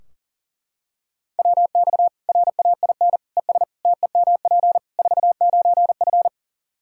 Morse code, heard as a keyed tone, about 40 wpm, 700 Hz. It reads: WX RAIN ES TEMP 49F
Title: Practice QSO - Exchange 2 - Info - 40wpm